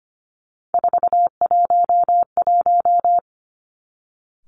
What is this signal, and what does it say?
Morse code, heard as a keyed tone, about 25 wpm, 700 Hz. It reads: 411
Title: Three Challenging Letter-Number Combinations 25wpm